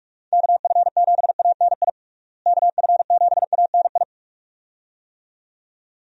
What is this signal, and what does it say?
Morse code, heard as a keyed tone, about 46 wpm, 700 Hz. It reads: KF7ANI KF7ANI
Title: USA Call Signs - 6 Characters - Repeated 2x - 45wpm